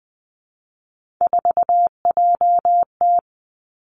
Morse code, heard as a keyed tone, about 20 wpm, 700 Hz. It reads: 4JT
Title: Three Challenging Letter-Number Combinations 20wpm